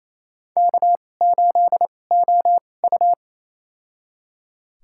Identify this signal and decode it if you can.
Morse code, heard as a keyed tone, about 28 wpm, spaced as for 20 wpm, 700 Hz. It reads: K8OU
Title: USA Call Signs - 4 Characters - 2x Character Spacing - 28wpm